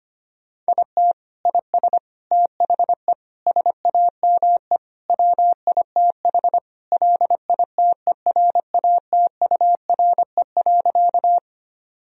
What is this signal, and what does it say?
Morse code, heard as a keyed tone, about 25 wpm, 700 Hz. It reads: IT IS THE SAME WITH LITERATURE.
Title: Sentences from Top 1000 Words - Review Part 3 - 25wpm